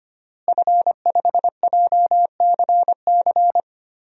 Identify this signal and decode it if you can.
Morse code, heard as a keyed tone, about 25 wpm, 700 Hz. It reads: F5JCC
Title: DXCC-100 Call Signs - 5 Characters - Repeated 3x - 25wpm